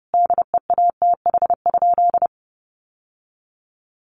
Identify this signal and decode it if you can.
Morse code, heard as a keyed tone, about 30 wpm, 700 Hz. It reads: DEATH?
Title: Sentences from Top 1100 Words 30wpm